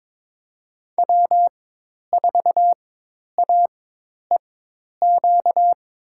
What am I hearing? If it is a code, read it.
Morse code, heard as a keyed tone, about 22 wpm, spaced as for 10 wpm, 700 Hz. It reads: W4AEQ
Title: USA Call Signs - 5 Characters - 4x Character Spacing - 22wpm